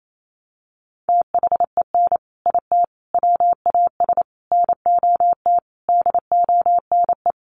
Morse code, heard as a keyed tone, about 28 wpm, 700 Hz. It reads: THEN IT WAS NOT DONE
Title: Sentences from Top 1300 Words 28wpm